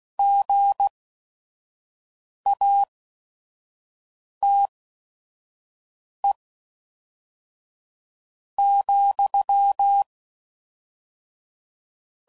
Morse code, heard as a keyed tone, about 16 wpm, spaced as for 4 wpm, 800 Hz. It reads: GATE,
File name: CW Code Groups @5WPM / Technical Subjects / AND Gate